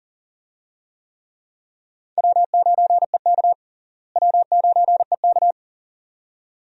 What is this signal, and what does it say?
Morse code, heard as a keyed tone, about 38 wpm, 700 Hz. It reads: W9EK W9EK
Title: USA Call Signs - 4 Characters - Repeated 2x - 40wpm